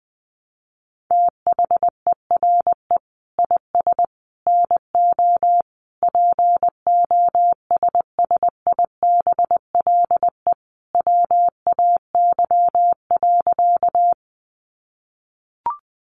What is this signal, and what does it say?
Morse code, heard as a keyed tone, about 20 wpm, 700 Hz. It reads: THERE IS NO POSSIBLE WAY.
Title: Sentences from Top 500 Words 20wpm